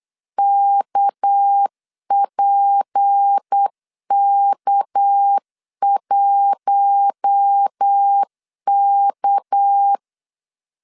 Morse code, heard as a keyed tone, about 8 wpm, 800 Hz. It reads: KPK1K
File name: LEZIONI / 6 PKX0 / 6 PK 8wpm 3003